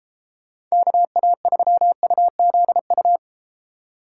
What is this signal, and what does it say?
Morse code, heard as a keyed tone, about 33 wpm, 700 Hz. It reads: KA3UZU